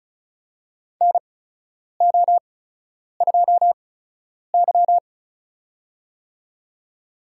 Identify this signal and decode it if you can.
Morse code, heard as a keyed tone, about 36 wpm, spaced as for 10 wpm, 700 Hz. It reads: NO2Y
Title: USA Call Signs - 4 Characters - 8x Character Spacing - 35wpm